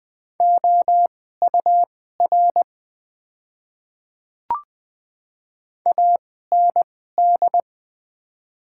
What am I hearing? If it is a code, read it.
Morse code, heard as a keyed tone, about 20 wpm, spaced as for 15 wpm, 700 Hz. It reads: OUR AND
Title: 2x Character Spacing - Top 300 Words - 20wpm